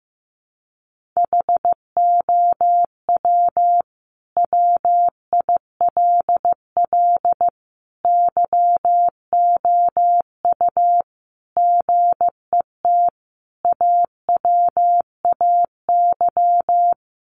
Morse code, heard as a keyed tone, about 14 wpm, 700 Hz. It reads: HOW WILL YOU GET AWAY
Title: Sentences from Top 1500 Words 15wpm